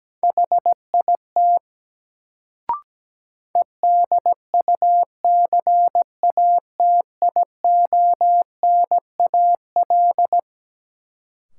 Morse code, heard as a keyed tone, about 17 wpm, 700 Hz. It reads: HIT EDUCATIONAL
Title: Sentences from Top 1200 Words - New Word Drill - 17wpm